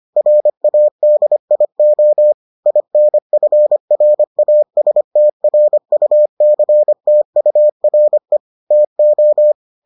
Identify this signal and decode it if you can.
Morse code, heard as a keyed tone, about 25 wpm, 600 Hz. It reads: RADIO INFRASTRUCTURE TO